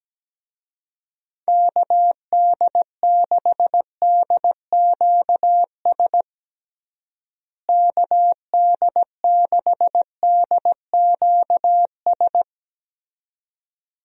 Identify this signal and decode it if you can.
Morse code, heard as a keyed tone, about 17 wpm, 700 Hz. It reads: KD6DQS KD6DQS
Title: USA Call Signs - 6 Characters - Repeated 2x - 17wpm